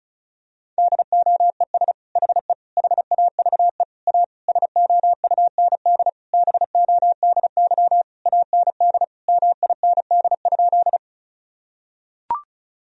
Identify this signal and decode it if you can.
Morse code, heard as a keyed tone, about 34 wpm, 700 Hz. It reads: DOES HE HAVE A SOUND BODY AND MIND?
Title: Sentences from Top 1000 Words - Review Part 2 - 35wpm